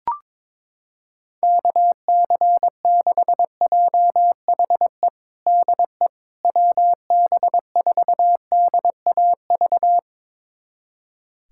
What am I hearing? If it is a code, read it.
Morse code, heard as a keyed tone, about 22 wpm, 700 Hz. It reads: KC6JHE DE WB4DAV
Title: USA Callsign DE Callsign 22wpm